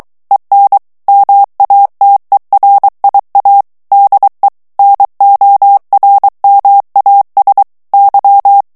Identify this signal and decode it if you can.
Morse code, heard as a keyed tone, about 23 wpm, 800 Hz. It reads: EN MATERIA DE NORMAS Y
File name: morse mp3 / ESPAGNOL TEXTE EN CLAIR 2 VITESSE 22 MOTS MINUTES